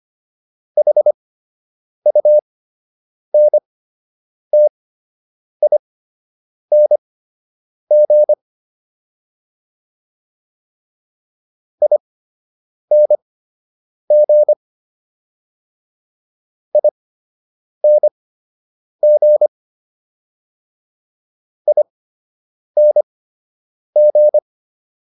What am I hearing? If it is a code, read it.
Morse code, heard as a keyed tone, about 25 wpm, spaced as for 8 wpm, 600 Hz. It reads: HUNTING ING ING ING